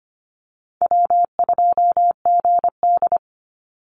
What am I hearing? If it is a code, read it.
Morse code, heard as a keyed tone, about 25 wpm, 700 Hz. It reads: W2GD
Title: Call Signs - 25wpm